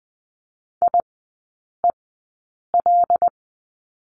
Morse code, heard as a keyed tone, about 20 wpm, spaced as for 8 wpm, 700 Hz. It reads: IEL
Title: 121 Lesson 20e - Number 3 - Three Characters 20wpm